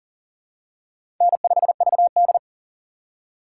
Morse code, heard as a keyed tone, about 40 wpm, 700 Hz. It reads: N5VD
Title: USA Call Signs - 4 Characters - 3x Character Spacing - 40wpm